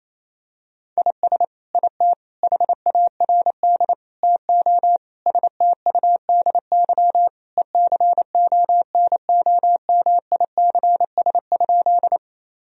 Morse code, heard as a keyed tone, about 28 wpm, 700 Hz. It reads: IS IT HARD TO STUDY ECONOMICS?
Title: Sentences from Top 1000 Words - Review Part 3 - 28wpm